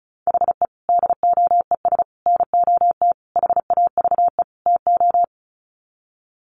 Morse code, heard as a keyed tone, about 34 wpm, 700 Hz. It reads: HE DOES NOT HAVE TO
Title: Sets of 5 Words Encore 35wpm